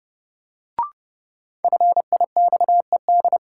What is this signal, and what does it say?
Morse code, heard as a keyed tone, about 30 wpm, 700 Hz. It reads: FIXED
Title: Sentences from Top 1100 Words - New Word Drill Alt - 30wpm